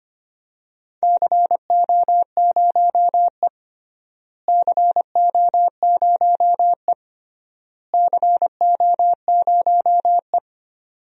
Morse code, heard as a keyed tone, about 25 wpm, 700 Hz. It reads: CO0E CO0E CO0E
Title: DXCC-100 Call Signs - 4 Characters - Repeated 3x - 25wpm